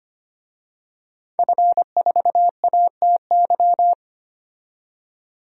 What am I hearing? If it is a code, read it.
Morse code, heard as a keyed tone, about 25 wpm, 700 Hz. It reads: F4ATY
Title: DXCC-100 Call Signs - 5 Characters - 25wpm